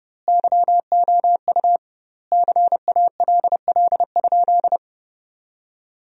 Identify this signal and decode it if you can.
Morse code, heard as a keyed tone, about 30 wpm, 700 Hz. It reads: YOU CALL?
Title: 2x Word-Spacing - Sentences from Top 100 Words - 30wpm